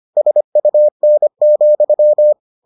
Morse code, heard as a keyed tone, about 25 wpm, 600 Hz. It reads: SUN,